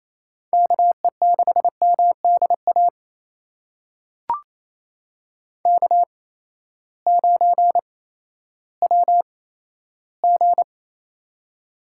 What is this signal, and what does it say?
Morse code, heard as a keyed tone, about 27 wpm, spaced as for 14 wpm, 700 Hz. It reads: KE6MDA K9WG
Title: USA Call Signs - 8x Character Spacing - 28wpm